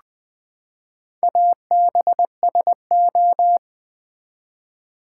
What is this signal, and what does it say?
Morse code, heard as a keyed tone, about 20 wpm, 700 Hz. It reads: ABSO
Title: Build A Word - 4 Letters At A Time - 8-Letter Words - 20wpm